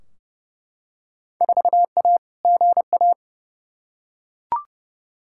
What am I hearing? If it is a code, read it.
Morse code, heard as a keyed tone, about 32 wpm, 700 Hz. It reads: 4A GA
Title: Field Day Exchanges - 30wpm